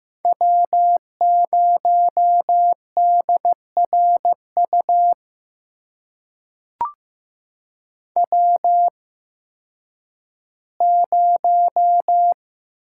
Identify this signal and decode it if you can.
Morse code, heard as a keyed tone, about 15 wpm, spaced as for 10 wpm, 700 Hz. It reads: W0DRU W0
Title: USA Call Signs - 5 Characters - 8x Character Spacing - 15wpm